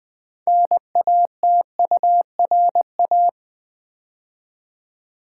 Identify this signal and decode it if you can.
Morse code, heard as a keyed tone, about 20 wpm, 700 Hz. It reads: NATURA
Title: Build A Word - 3 Letters At A Time - 9-Letter Words - 20wpm